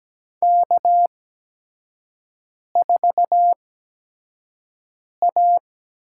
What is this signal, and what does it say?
Morse code, heard as a keyed tone, about 17 wpm, spaced as for 5 wpm, 700 Hz. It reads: K4A
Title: USA Call Signs - 5 Characters - 8x Character Spacing - 17wpm